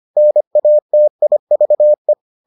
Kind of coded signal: Morse code

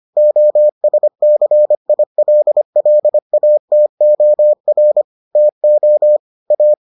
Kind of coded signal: Morse code

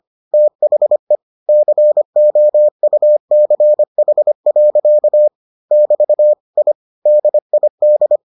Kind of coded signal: Morse code

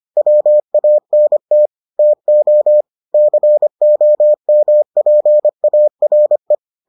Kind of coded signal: Morse code

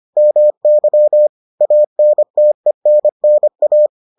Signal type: Morse code

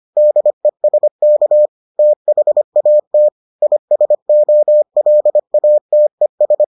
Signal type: Morse code

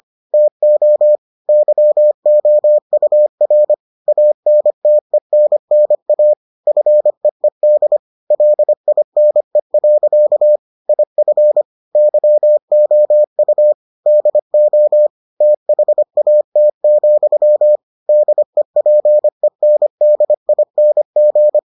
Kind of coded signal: Morse code

